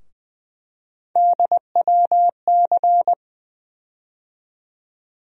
Morse code, heard as a keyed tone, about 21 wpm, 700 Hz. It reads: DWC
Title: USA Call Sign Postfixes - 3 Letters 20wpm